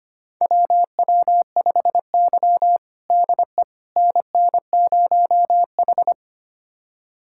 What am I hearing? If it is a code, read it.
Morse code, heard as a keyed tone, about 25 wpm, 700 Hz. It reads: WW5Y DE NN0H
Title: USA Callsign DE Callsign - 4 Characters - 25wpm